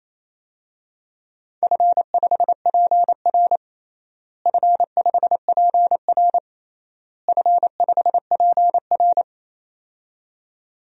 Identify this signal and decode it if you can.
Morse code, heard as a keyed tone, about 28 wpm, 700 Hz. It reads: F5PR F5PR F5PR
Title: DXCC-100 Call Signs - 4 Characters - Repeated 3x - 28wpm